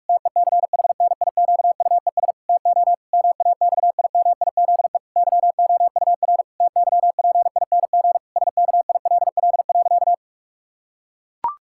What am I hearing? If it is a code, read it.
Morse code, heard as a keyed tone, about 41 wpm, 700 Hz. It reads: TECHNIQUES TO MAXIMIZE YOUR TYPING SKILL.
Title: Pangram Challenge v2 - 45wpm